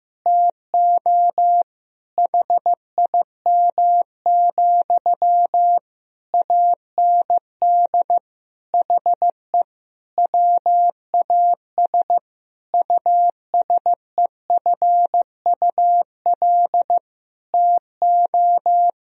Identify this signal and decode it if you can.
Morse code, heard as a keyed tone, about 15 wpm, 700 Hz. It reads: TO HIM, AND HE WAS USEFUL TO
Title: Sentences from Top 1000 Words - Review Part 4 - 15wpm